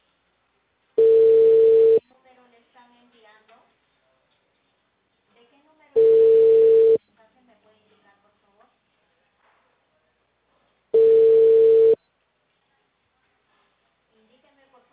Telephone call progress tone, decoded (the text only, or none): ringback tone